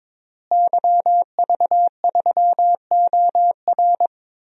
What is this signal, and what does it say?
Morse code, heard as a keyed tone, about 23 wpm, 700 Hz. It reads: YV3OR
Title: DXCC-100 Call Signs - 5 Characters - Repeated 3x - 22wpm